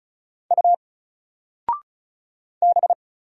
Morse code, heard as a keyed tone, about 36 wpm, 700 Hz. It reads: U B
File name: Single-Letter-Number-35wpm